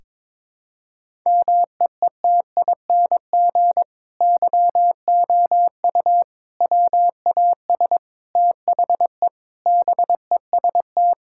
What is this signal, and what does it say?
Morse code, heard as a keyed tone, about 22 wpm, 700 Hz. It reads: MEETING YOU WAS THE BEST
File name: practice / 22wpm / Sentences with 8-Word Phrases - 22wpm